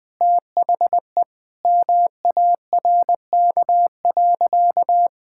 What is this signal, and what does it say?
Morse code, heard as a keyed tone, about 20 wpm, 700 Hz. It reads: THE MARK.
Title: Sentences from Top 1000 Words - Review Part 6 - 20wpm